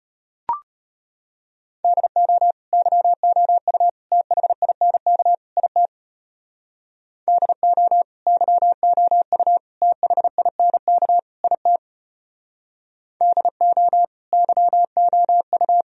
Morse code, heard as a keyed tone, about 34 wpm, 700 Hz. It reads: DO YOU THINK IT DO YOU THINK IT DO YOU